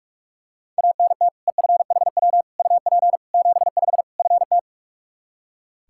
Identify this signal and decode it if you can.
Morse code, heard as a keyed tone, about 46 wpm, 700 Hz. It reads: ANT EFHW UP 75 FT
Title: K1USN SST QSO Practice 45wpm